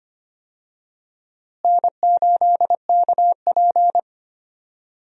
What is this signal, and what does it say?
Morse code, heard as a keyed tone, about 25 wpm, 700 Hz. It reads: N8KP
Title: USA Call Signs - 4 Characters - Repeated 2x - 25wpm